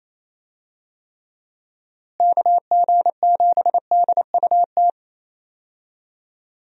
Morse code, heard as a keyed tone, about 28 wpm, 700 Hz. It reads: KG7DUT
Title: USA Call Signs - 6 Characters - Repeated 3x - 28wpm